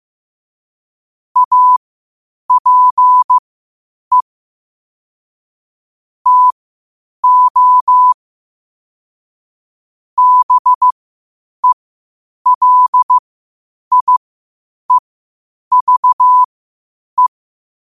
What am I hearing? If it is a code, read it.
Morse code, heard as a keyed tone, about 15 wpm, spaced as for 8 wpm, 1000 Hz. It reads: APE TO BELIEVE
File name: qotdfiles / 8wpm